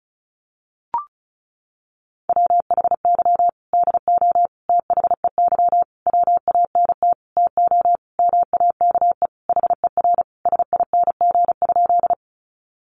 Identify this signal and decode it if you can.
Morse code, heard as a keyed tone, about 34 wpm, 700 Hz. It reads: WHY DO THEY WANT TO MAKE HER SING?
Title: Sentences from Top 400 Words 35wpm